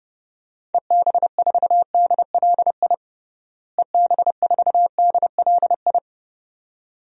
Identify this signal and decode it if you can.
Morse code, heard as a keyed tone, about 32 wpm, 700 Hz. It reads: EB4DLI EB4DLI
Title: DXCC-100 Call Signs - 6 Characters - Repeated 2x - 30wpm